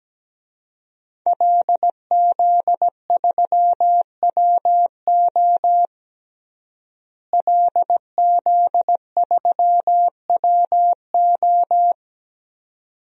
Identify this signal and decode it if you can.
Morse code, heard as a keyed tone, about 17 wpm, 700 Hz. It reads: LZ3WO LZ3WO
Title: DXCC-100 Call Signs - 5 Characters - Repeated 2x - 17wpm